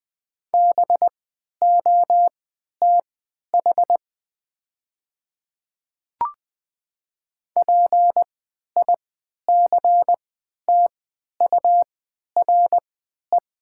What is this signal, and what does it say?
Morse code, heard as a keyed tone, about 20 wpm, spaced as for 11 wpm, 700 Hz. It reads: BOTH PICTURE
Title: 3x Character Spacing - Top 300 Words - 20wpm